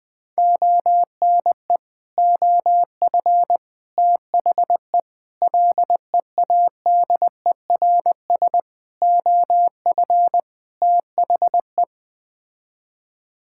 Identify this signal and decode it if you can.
Morse code, heard as a keyed tone, about 21 wpm, 700 Hz. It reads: ONE OF THE LEADERS OF THE
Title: Sets of 6 Words - 20wpm